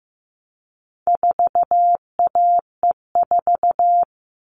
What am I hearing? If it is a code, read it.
Morse code, heard as a keyed tone, about 15 wpm, 700 Hz. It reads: 4AE4
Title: All Cut Numbers - 4-Digit 15wpm